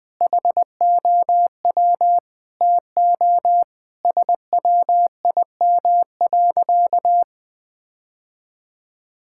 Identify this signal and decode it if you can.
Morse code, heard as a keyed tone, about 19 wpm, 700 Hz. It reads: HOW TO SWIM.
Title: Sentences with 5-Word Phrases - 20wpm